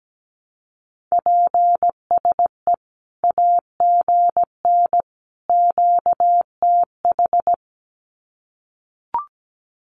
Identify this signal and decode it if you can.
Morse code, heard as a keyed tone, about 17 wpm, 700 Hz. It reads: PSE AGN QTH
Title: Common QSO Phrases 17wpm